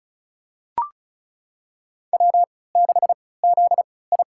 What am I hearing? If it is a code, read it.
Morse code, heard as a keyed tone, about 34 wpm, spaced as for 20 wpm, 700 Hz. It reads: W6ZI
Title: USA Call Signs - 4 Characters - 3x Character Spacing - 35wpm